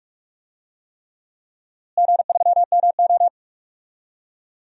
Morse code, heard as a keyed tone, about 41 wpm, 700 Hz. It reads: G3MO
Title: DXCC-100 Call Signs - 4 Characters - Repeated 3x - 45wpm